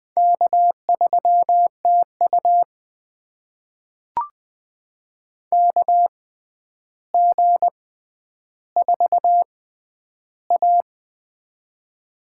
Morse code, heard as a keyed tone, about 20 wpm, spaced as for 10 wpm, 700 Hz. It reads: K3TU KG4A
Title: USA Call Signs - 4 Characters - 6x Character Spacing - 20wpm